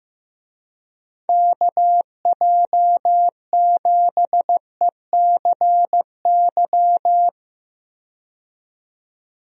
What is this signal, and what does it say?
Morse code, heard as a keyed tone, about 15 wpm, 700 Hz. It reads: KJ7ECY
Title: USA Call Signs - 6 Characters - 15wpm